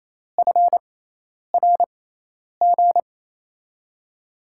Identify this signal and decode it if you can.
Morse code, heard as a keyed tone, about 28 wpm, spaced as for 10 wpm, 700 Hz. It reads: FRG
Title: USA Call Signs - 6 Characters - 6x Character Spacing - 28wpm